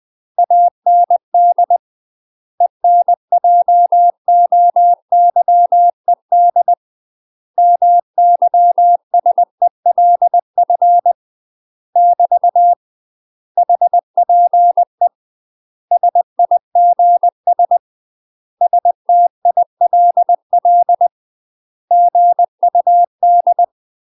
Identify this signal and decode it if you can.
Morse code, heard as a keyed tone, about 20 wpm, 700 Hz. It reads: AND ENJOYED MYSELF <BT> HPE SIGS STILL GUD
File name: UK MASTER QSOs TRAINING / 20WPM / UK MASTER QSOs 0001